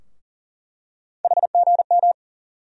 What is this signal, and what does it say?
Morse code, heard as a keyed tone, about 40 wpm, 700 Hz. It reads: HGM
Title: USA Call Sign Postfixes - 3 Letters 40wpm